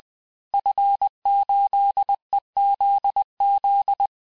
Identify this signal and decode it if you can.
Morse code, heard as a keyed tone, about 20 wpm, 800 Hz. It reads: F8EZZ
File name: cours-du-mercredi / audios / cours-870-FIN-de-QSO-vitesse-19-MPM